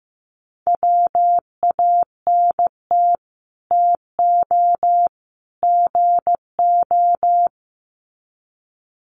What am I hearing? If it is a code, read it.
Morse code, heard as a keyed tone, about 15 wpm, 700 Hz. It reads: WANT TO GO
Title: Sets of 3 Words 15wpm